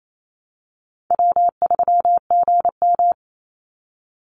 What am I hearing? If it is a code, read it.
Morse code, heard as a keyed tone, about 28 wpm, 700 Hz. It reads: W3GM
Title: USA Call Signs 28wpm